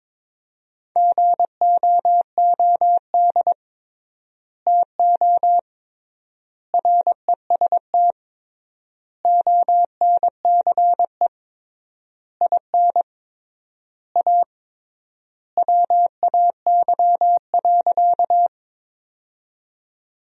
Morse code, heard as a keyed tone, about 22 wpm, 700 Hz. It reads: GOOD TO REST ONCE IN A WAY.
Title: 3x Word-Spacing - Sentences from Top 500 Words - 22wpm